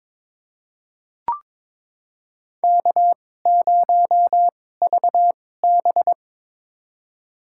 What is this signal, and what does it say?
Morse code, heard as a keyed tone, about 22 wpm, spaced as for 16 wpm, 700 Hz. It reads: K0VB
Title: USA Call Signs - 4 Characters - 2x Character Spacing - 22wpm